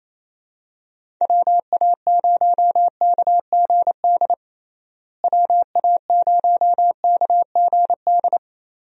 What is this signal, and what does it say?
Morse code, heard as a keyed tone, about 28 wpm, 700 Hz. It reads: WA0KGD WA0KGD
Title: USA Call Signs - 6 Characters - Repeated 2x - 28wpm